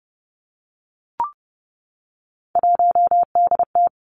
Morse code, heard as a keyed tone, about 30 wpm, 700 Hz. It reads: 1DT
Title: Three Challenging Letter-Number Combinations 30wpm